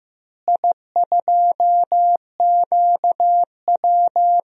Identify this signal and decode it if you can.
Morse code, heard as a keyed tone, about 15 wpm, 700 Hz. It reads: I2QW
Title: DXCC-100 Call Signs - 4 Characters - Repeated 3x - 15wpm